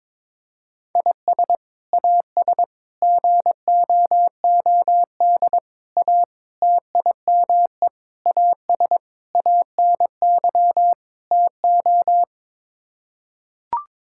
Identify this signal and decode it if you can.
Morse code, heard as a keyed tone, about 22 wpm, 700 Hz. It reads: IS AS GOOD A TIME AS ANY TO
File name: practice / 22wpm / Sets of 8 Words - 22wpm